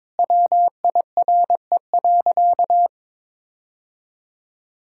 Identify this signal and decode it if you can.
Morse code, heard as a keyed tone, about 22 wpm, 700 Hz. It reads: WIRE.
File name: practice / 22wpm / Sentences with 3-Word Phrases - 22wpm